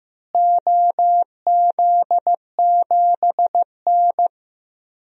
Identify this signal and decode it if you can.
Morse code, heard as a keyed tone, about 15 wpm, 700 Hz. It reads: OZ7N
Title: DXCC-100 Call Signs - 4 Characters - Repeated 3x - 15wpm